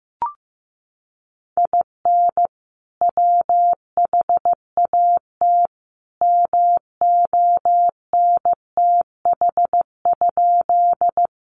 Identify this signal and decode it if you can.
Morse code, heard as a keyed tone, about 15 wpm, 700 Hz. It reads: IN WHAT MONTH?
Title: Sentences from Top 800 Words 15wpm